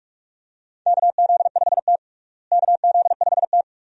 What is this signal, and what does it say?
Morse code, heard as a keyed tone, about 47 wpm, 700 Hz. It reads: KZ5T KZ5T
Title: USA Call Signs - 4 Characters - Repeated 2x - 45wpm